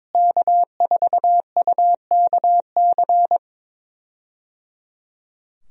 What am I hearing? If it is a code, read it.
Morse code, heard as a keyed tone, about 22 wpm, 700 Hz. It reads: K4UKC
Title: USA Call Signs - 5 Characters - Repeated 3x - 22wpm